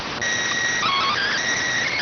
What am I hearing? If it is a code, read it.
SSTV (slow-scan television), PD 290.